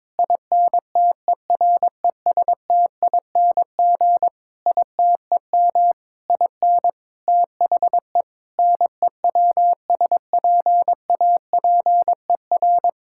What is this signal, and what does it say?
Morse code, heard as a keyed tone, about 22 wpm, 700 Hz. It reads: INTERESTING ITEM IN THE NEWSPAPER
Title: Sentences from Top 1000 Words - Review Part 6 - 22wpm